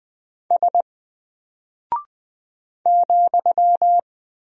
Morse code, heard as a keyed tone, about 20 wpm, 700 Hz. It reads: S ,
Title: Single Character (Letter:Number:Punctuation:Prosign) Alt 20wpm